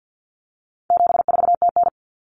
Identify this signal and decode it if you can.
Morse code, heard as a keyed tone, about 50 wpm, 700 Hz. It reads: 74TN
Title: Common Cut Numbers - 4-Digit 50wpm